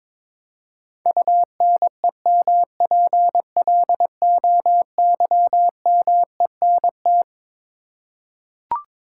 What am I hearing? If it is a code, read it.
Morse code, heard as a keyed tone, about 21 wpm, 700 Hz. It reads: UNEMPLOYMENT
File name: Twelve Letter Words - 22wpm